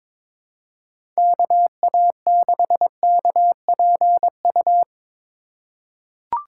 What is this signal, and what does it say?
Morse code, heard as a keyed tone, about 22 wpm, 700 Hz. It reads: KA6KPU
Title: USA Call Signs - 6 Characters - 2x Character Spacing - 22wpm